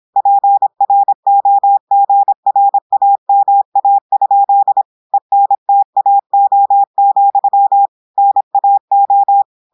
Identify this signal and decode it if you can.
Morse code, heard as a keyed tone, about 26 wpm, 800 Hz. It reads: PROGRAMA? ENTAO, NAO
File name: portugais texte en clair 2 vitesse 26 mots minutes